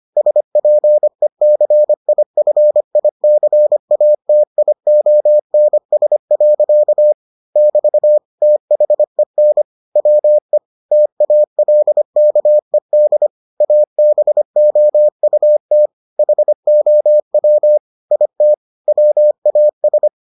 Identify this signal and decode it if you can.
Morse code, heard as a keyed tone, about 25 wpm, 600 Hz. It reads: SPECIFICATIONS. <BT> THEN WE TALKED ABOUT HOW IT WAS